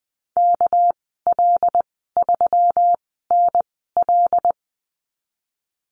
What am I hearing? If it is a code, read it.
Morse code, heard as a keyed tone, about 20 wpm, spaced as for 14 wpm, 700 Hz. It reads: KL3NL
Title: 200 Lesson 32g - Letter k - Call Signs 20wpm